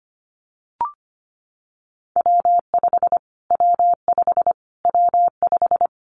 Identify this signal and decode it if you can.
Morse code, heard as a keyed tone, about 25 wpm, 700 Hz. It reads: W5 W5 W5
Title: Callsign Prefixes for Top 25 DXCC - Repeated 3x - 25wpm